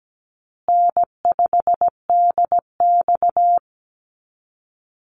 Morse code, heard as a keyed tone, about 17 wpm, 700 Hz. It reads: N5DX
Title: Call Signs Repeated 3x - 17wpm